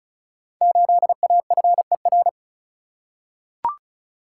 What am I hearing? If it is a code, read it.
Morse code, heard as a keyed tone, about 36 wpm, 690 Hz. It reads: 8AFER
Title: Code Groups - 5 Letter:Numbers - 35wpm